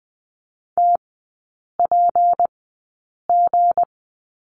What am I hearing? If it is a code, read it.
Morse code, heard as a keyed tone, about 20 wpm, spaced as for 8 wpm, 690 Hz. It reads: TPG
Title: 159 Lesson 26e - Letter g - Three Characters 20wpm